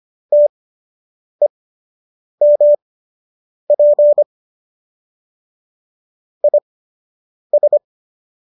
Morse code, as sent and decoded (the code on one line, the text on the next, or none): - . -- .--. / .. ...
TEMP IS